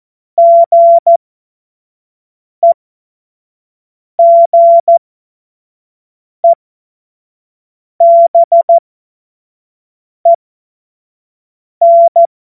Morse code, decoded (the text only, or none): GEGEBEN